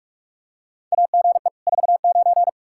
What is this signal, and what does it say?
Morse code, heard as a keyed tone, about 46 wpm, 700 Hz. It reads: AGE 49
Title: K1USN SST QSO Practice 45wpm